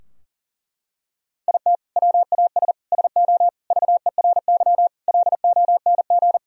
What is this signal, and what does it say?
Morse code, heard as a keyed tone, about 37 wpm, 700 Hz. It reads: IT WAS SO VERY LONG
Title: Sentences from Top 1000 Words - Review Part 3 - 40wpm